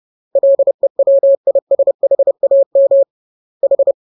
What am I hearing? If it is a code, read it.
Morse code, heard as a keyed tone, about 30 wpm, 550 Hz. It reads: LEWISHAM H
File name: jokes / joke 15 30 0000